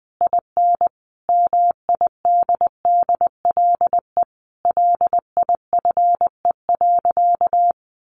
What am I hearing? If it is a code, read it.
Morse code, heard as a keyed tone, about 20 wpm, 700 Hz. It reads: IN MIDDLE LIFE.
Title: Sentences from Top 1300 Words 20wpm